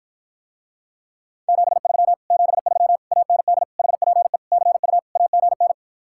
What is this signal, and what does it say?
Morse code, heard as a keyed tone, about 54 wpm, 700 Hz. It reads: 73 73 AND HPE CU AGN